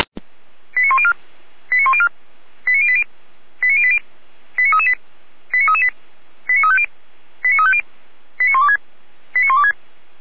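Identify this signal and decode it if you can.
Five-tone selective call (ZVEI), heard as quick runs of five tones, tones of about 70 ms. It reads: 89104 89104 89080 89080 89338 89338 89350 89350 89126 89126
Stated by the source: Five-tone ZVEI call received on an SDR.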